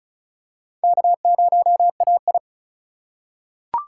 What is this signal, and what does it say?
Morse code, heard as a keyed tone, about 36 wpm, 700 Hz. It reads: K0AI
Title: USA Call Signs - 4 Characters - 4x Character Spacing - 35wpm